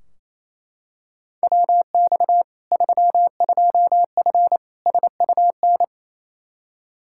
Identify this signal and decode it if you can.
Morse code, heard as a keyed tone, about 28 wpm, 700 Hz. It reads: WX 32F SUN
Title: Practice QSO - Exchange 2 - Info - 28wpm